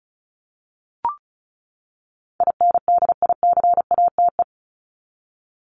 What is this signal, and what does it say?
Morse code, heard as a keyed tone, about 34 wpm, 700 Hz. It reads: INDICATE
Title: ED Suffix - Top 100 Words with Base Words 35wpm